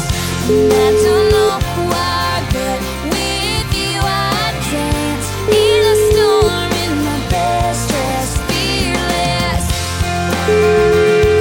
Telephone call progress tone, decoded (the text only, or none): ringback tone